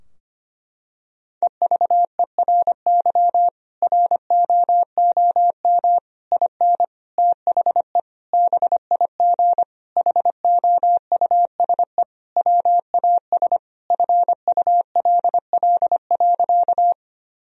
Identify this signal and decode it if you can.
Morse code, heard as a keyed tone, about 25 wpm, 700 Hz. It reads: EVERY ROOM IN THE BIG HOUSE WAS FULL.
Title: Sentences from Top 1000 Words - Review Part 3 - 25wpm